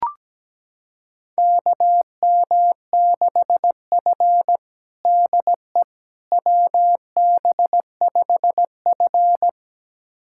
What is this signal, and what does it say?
Morse code, heard as a keyed tone, about 17 wpm, 700 Hz. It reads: KM6F DE WB5F
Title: USA Callsign DE Callsign - 4 Characters - 17wpm